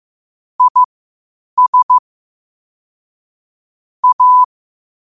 Morse code, heard as a keyed tone, about 15 wpm, spaced as for 8 wpm, 1000 Hz. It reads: IS A
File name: qotdfiles / 8wpm